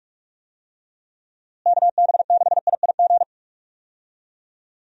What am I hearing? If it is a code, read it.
Morse code, heard as a keyed tone, about 41 wpm, 700 Hz. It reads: KB6IIG
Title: USA Call Signs - 6 Characters - Repeated 2x - 45wpm